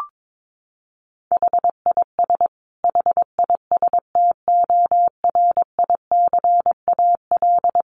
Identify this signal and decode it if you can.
Morse code, heard as a keyed tone, about 22 wpm, 700 Hz. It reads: HIS HISTORICAL
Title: Sentences from Top 1300 Words 22wpm